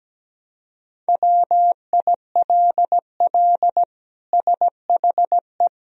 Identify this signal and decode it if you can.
Morse code, heard as a keyed tone, about 17 wpm, 700 Hz. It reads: WILL SHE
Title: Sentences from Top 1000 Words - Review Part 3 - 17wpm